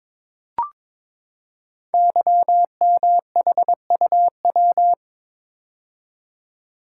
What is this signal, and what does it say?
Morse code, heard as a keyed tone, about 22 wpm, 700 Hz. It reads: YMHUW
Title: Code Groups - 5 Letter:Numbers - 22wpm